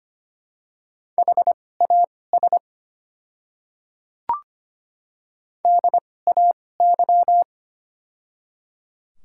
Morse code, heard as a keyed tone, about 25 wpm, spaced as for 18 wpm, 700 Hz. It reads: HAS DAY
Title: 2x Character Spacing - Top 100 Words - 25wpm